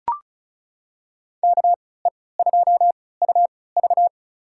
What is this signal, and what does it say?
Morse code, heard as a keyed tone, about 34 wpm, spaced as for 20 wpm, 700 Hz. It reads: KE2UV
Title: USA Call Signs - 5 Characters - 3x Character Spacing - 35wpm